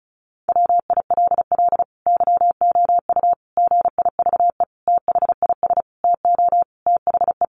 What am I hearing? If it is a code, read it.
Morse code, heard as a keyed tone, about 34 wpm, 700 Hz. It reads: WILL YOU GIVE THIS TO THE
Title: Sentences from Top 200 Words 35wpm